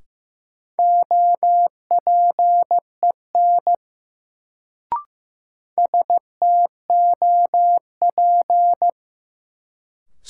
Morse code, heard as a keyed tone, about 15 wpm, 700 Hz. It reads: OPEN STOP
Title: Sentences from Top 300 Words - New Word Drill - 15wpm